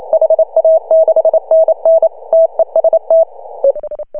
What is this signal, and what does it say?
Morse code, heard as a keyed tone, about 28 wpm, 650 Hz. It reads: HA6NN TEST
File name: CW / HA2OS RHOB 2017 CW